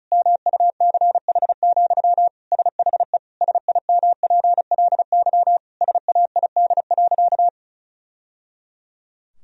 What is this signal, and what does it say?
Morse code, heard as a keyed tone, about 36 wpm, 700 Hz. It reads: MUCH, SHE SIMPLY SAID.
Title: Sentences from Top 1000 Words - Review Part 4 - 35wpm